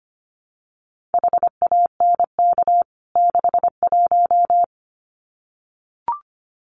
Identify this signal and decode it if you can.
Morse code, heard as a keyed tone, about 25 wpm, 700 Hz. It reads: HANK 61
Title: CWT Exchange 25wpm